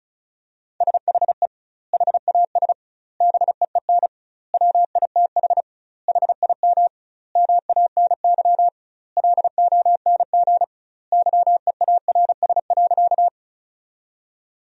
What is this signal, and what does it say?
Morse code, heard as a keyed tone, about 34 wpm, 700 Hz. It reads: SHE HAS BEEN WITH HIM MANY LONG YEARS.
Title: 2x Word-Spacing - Sentences from Top 200 Words - 35wpm